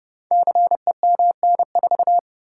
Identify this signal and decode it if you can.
Morse code, heard as a keyed tone, about 30 wpm, 700 Hz. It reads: CEMN4
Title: Code Groups - 5 Letter:Numbers - 30wpm